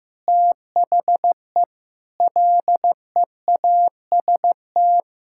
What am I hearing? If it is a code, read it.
Morse code, heard as a keyed tone, about 15 wpm, 700 Hz. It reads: THE LEAST
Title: Sentences from Top 1000 Words - Review Part 6 - 15wpm